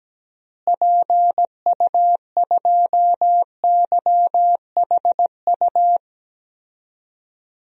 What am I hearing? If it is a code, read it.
Morse code, heard as a keyed tone, about 17 wpm, 700 Hz. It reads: PU2YHU
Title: DXCC-100 Call Signs - 6 Characters - Repeated 2x - 17wpm